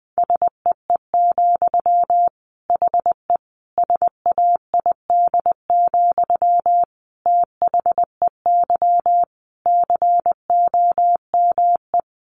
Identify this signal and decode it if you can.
Morse code, heard as a keyed tone, about 20 wpm, 700 Hz. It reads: SEE, HE SAID, THEY COME
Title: Sentences from Top 700 Words 20wpm